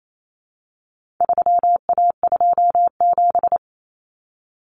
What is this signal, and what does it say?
Morse code, heard as a keyed tone, about 28 wpm, 700 Hz. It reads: 3A27
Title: Common Cut Numbers - 4-Digit 28wpm